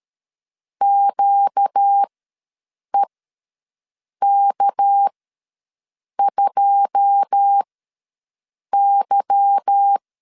Morse code, heard as a keyed tone, about 13 wpm, spaced as for 6 wpm, 800 Hz. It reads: QEK2Y
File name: LEZIONI / 9 2 ws 12wpm 3406